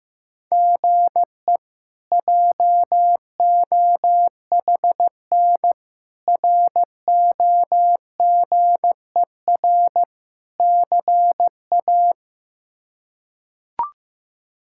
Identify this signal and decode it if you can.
Morse code, heard as a keyed tone, about 15 wpm, 700 Hz. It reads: GE JOHN ROGER CA
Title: K1USN SST QSO Practice - 15wpm